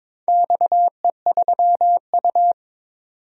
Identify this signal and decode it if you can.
Morse code, heard as a keyed tone, about 22 wpm, 700 Hz. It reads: XE3U
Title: DXCC-100 Call Signs - 4 Characters - 22wpm